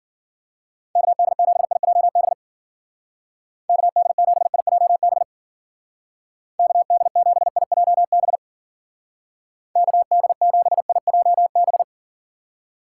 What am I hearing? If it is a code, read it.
Morse code, heard as a keyed tone, about 49 wpm, 700 Hz. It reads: KD7IJB KD7IJB KD7IJB KD7IJB
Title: Speed Racing - USA 2x3 Call Signs 40wpm